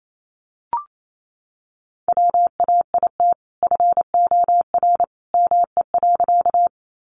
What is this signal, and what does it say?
Morse code, heard as a keyed tone, about 28 wpm, 700 Hz. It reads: WAIT FOR ME.
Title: Sentences from Top 500 Words 28wpm